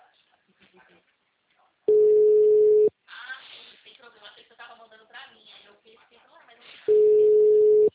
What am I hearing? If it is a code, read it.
Ringback tone (Europe, 425 Hz), with speech.